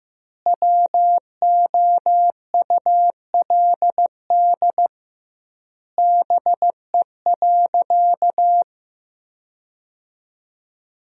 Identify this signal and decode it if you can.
Morse code, heard as a keyed tone, about 15 wpm, 700 Hz. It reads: WOULD BE.
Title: 2x Word-Spacing - Sentences from Top 200 Words - 15wpm